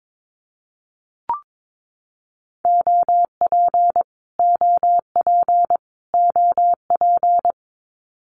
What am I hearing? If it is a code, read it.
Morse code, heard as a keyed tone, about 22 wpm, 700 Hz. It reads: OP OP OP
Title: Abbreviations Repeated 3x - 22wpm